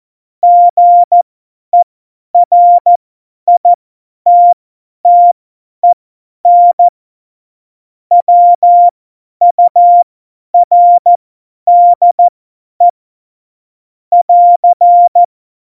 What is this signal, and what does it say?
Morse code, heard as a keyed tone, about 14 wpm, spaced as for 10 wpm, 700 Hz. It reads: GERITTEN WURDE <AR>
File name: Morsekurs / Witze / Witz 12 Mai 2020, 50 F70